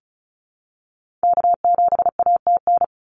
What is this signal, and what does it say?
Morse code, heard as a keyed tone, about 36 wpm, 700 Hz. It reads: K7ATN
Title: NA SOTA Activator Callsigns 35wpm